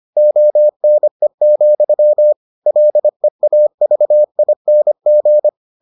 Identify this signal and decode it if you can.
Morse code, heard as a keyed tone, about 25 wpm, 600 Hz. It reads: ONE, LEAVING